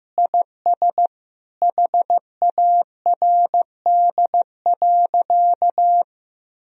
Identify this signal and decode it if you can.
Morse code, heard as a keyed tone, about 15 wpm, 700 Hz. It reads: IS HARD.
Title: Sentences from Top 1000 Words - Review Part 3 - 15wpm